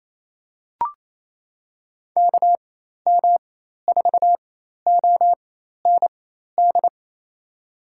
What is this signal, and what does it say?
Morse code, heard as a keyed tone, about 28 wpm, spaced as for 13 wpm, 700 Hz. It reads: KM4OND
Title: USA Call Signs - 6 Characters - 4x Character Spacing - 28wpm